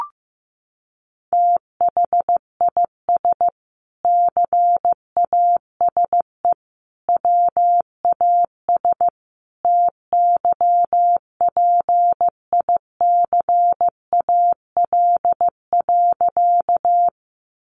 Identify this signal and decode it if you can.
Morse code, heard as a keyed tone, about 15 wpm, 700 Hz. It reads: THIS CASE WAS TYPICAL.
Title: Sentences from Top 1100 Words 15wpm